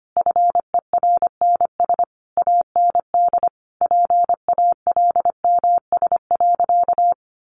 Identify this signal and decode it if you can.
Morse code, heard as a keyed tone, about 25 wpm, 700 Hz. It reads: FERNS AND PALMS.
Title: A Sound Of Thunder 25wpm